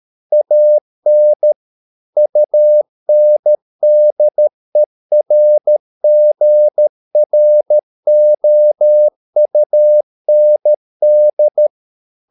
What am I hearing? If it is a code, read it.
Morse code, heard as a keyed tone, about 13 wpm, 600 Hz. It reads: AN UNDERGROUND